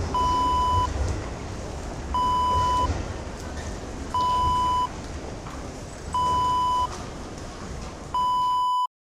There is no code.